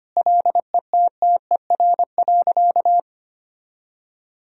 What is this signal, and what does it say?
Morse code, heard as a keyed tone, about 25 wpm, 700 Hz. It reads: LETTER.
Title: Sentences from Top 1000 Words - Review Part 1 - 25wpm